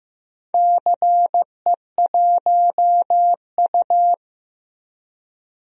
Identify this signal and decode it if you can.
Morse code, heard as a keyed tone, about 15 wpm, 700 Hz. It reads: CE1U
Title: DXCC-100 Call Signs - 4 Characters - Repeated 3x - 15wpm